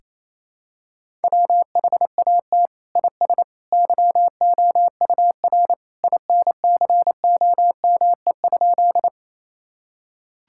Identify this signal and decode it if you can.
Morse code, heard as a keyed tone, about 28 wpm, 700 Hz. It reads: WHAT IS YOUR INCOME?
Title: Sentences from Top 1000 Words - Review Part 5 - 28wpm